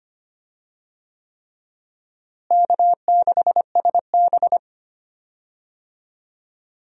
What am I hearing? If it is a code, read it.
Morse code, heard as a keyed tone, about 25 wpm, 700 Hz. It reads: K6SB
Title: USA Call Signs - 4 Characters - Repeated 3x - 25wpm